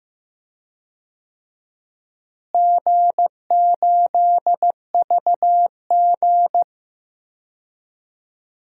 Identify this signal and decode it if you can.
Morse code, heard as a keyed tone, about 15 wpm, 700 Hz. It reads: G8VG
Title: DXCC-100 Call Signs - 4 Characters - Repeated 3x - 15wpm